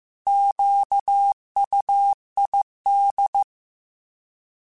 Morse code, heard as a keyed tone, about 15 wpm, 800 Hz. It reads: QUID
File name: Alphabetical / Alphabetical mp3s / Qs @13